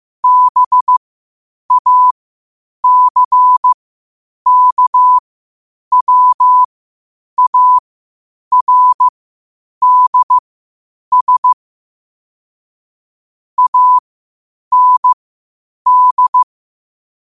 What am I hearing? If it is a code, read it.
Morse code, heard as a keyed tone, about 15 wpm, spaced as for 8 wpm, 1000 Hz. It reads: BACKWARDS AND
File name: qotdfiles / 8wpm